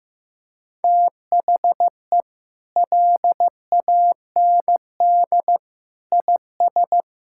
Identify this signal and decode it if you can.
Morse code, heard as a keyed tone, about 15 wpm, 700 Hz. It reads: THE LAND IS
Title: Sentences from Top 1000 Words - Review Part 6 - 15wpm